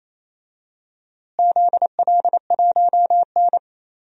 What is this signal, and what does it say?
Morse code, heard as a keyed tone, about 28 wpm, 700 Hz. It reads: ZL1N
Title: DXCC-100 Call Signs - 4 Characters - Repeated 3x - 28wpm